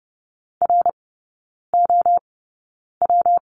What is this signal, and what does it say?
Morse code, heard as a keyed tone, about 30 wpm, spaced as for 9 wpm, 700 Hz. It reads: ROW
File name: Lesson 19e - Letter w - Three Characters 30wpm